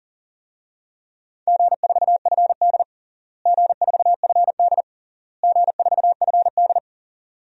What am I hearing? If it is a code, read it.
Morse code, heard as a keyed tone, about 40 wpm, 700 Hz. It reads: G4FD G4FD G4FD
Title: DXCC-100 Call Signs - 4 Characters - Repeated 3x - 40wpm